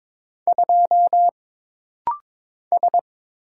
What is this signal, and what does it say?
Morse code, heard as a keyed tone, about 22 wpm, 700 Hz. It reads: 2 S
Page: This rapid-fire timing is geared to push your ICR (Instant Character Recognition) skills.